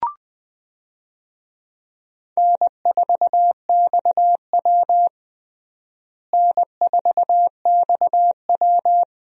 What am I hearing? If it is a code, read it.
Morse code, heard as a keyed tone, about 20 wpm, 700 Hz. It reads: N4XW N4XW
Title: USA Call Signs - 4 Characters - Repeated 2x - 20wpm